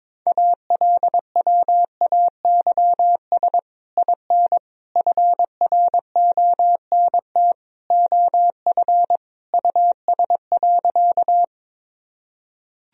Morse code, heard as a keyed tone, about 22 wpm, 700 Hz. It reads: ALWAYS IN FRONT OF US.
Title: Sentences from Top 1000 Words - Review Part 2 - 22wpm